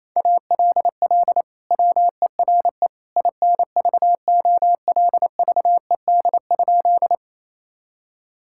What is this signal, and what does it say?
Morse code, heard as a keyed tone, about 28 wpm, 700 Hz. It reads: ALL WERE INVOLVED?
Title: Sentences from Top 1000 Words - Review Part 5 - 28wpm